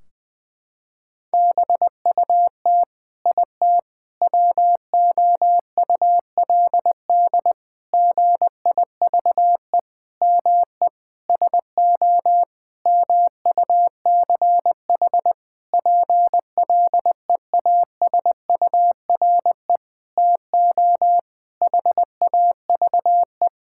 Morse code, heard as a keyed tone, about 20 wpm, 700 Hz. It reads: BUT IT WOULD GIVE ME SO MUCH PLEASURE TO HAVE
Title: Sentences from Top 1600 Words 20wpm